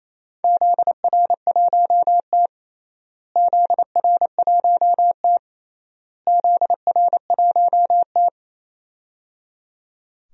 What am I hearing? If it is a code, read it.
Morse code, heard as a keyed tone, about 28 wpm, 700 Hz. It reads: ZR1T ZR1T ZR1T
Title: DXCC-100 Call Signs - 4 Characters - Repeated 3x - 28wpm